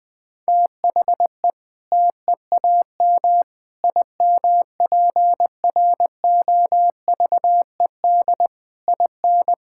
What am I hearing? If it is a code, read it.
Morse code, heard as a keyed tone, about 20 wpm, 700 Hz. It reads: THE TEAM IMPROVED IN
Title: Sentences with 7-Word Phrases - 20wpm